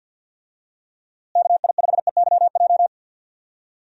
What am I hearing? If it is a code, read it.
Morse code, heard as a keyed tone, about 51 wpm, 700 Hz. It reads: KI5EYJ
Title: USA Call Signs - 6 Characters - Repeated 2x - 50wpm